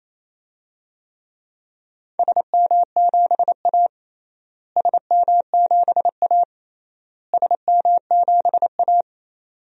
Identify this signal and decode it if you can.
Morse code, heard as a keyed tone, about 28 wpm, 700 Hz. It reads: SM7A SM7A SM7A
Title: DXCC-100 Call Signs - 4 Characters - Repeated 3x - 28wpm